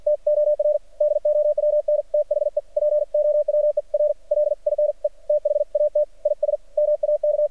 Morse code, heard as a keyed tone, about 53 wpm, 600 Hz. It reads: T OR DOWN THE POWER LINE THAT IS MAK
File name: archived / CRSnet